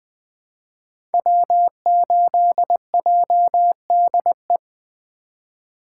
Morse code, heard as a keyed tone, about 20 wpm, 700 Hz. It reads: W8JDE
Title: 2023 POTA Activator Call Signs - 20wpm